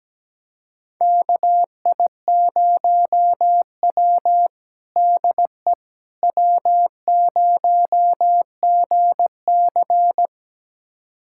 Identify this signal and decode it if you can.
Morse code, heard as a keyed tone, about 17 wpm, 700 Hz. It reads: KI0W DE W0GC
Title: USA Callsign DE Callsign - 4 Characters - 17wpm